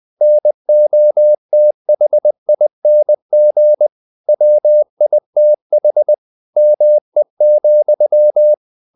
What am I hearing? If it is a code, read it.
Morse code, heard as a keyed tone, about 20 wpm, 600 Hz. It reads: NOTHING WITH ME,
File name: the old man and the sea / the old man and the sea part 1 20wpm